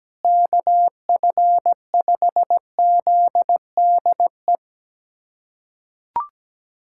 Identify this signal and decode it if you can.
Morse code, heard as a keyed tone, about 17 wpm, 700 Hz. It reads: KF5ZDE
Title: USA Callsign DE Callsign 17wpm